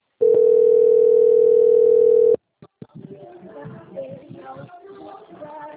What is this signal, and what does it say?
Ringback tone (North America), 440 + 480 Hz.